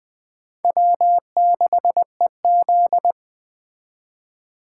Morse code, heard as a keyed tone, about 20 wpm, 700 Hz. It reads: W6EZ
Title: Speed Racing - USA 1x2 Call Signs 17wpm